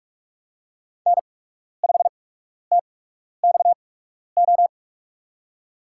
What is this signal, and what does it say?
Morse code, heard as a keyed tone, about 44 wpm, spaced as for 12 wpm, 690 Hz. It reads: N5TXO
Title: USA Call Signs - 5 Characters - 8x Character Spacing - 45wpm